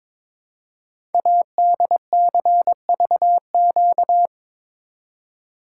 Morse code, heard as a keyed tone, about 22 wpm, 700 Hz. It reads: ADCVQ
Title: Code Groups - 5 Letters - 22wpm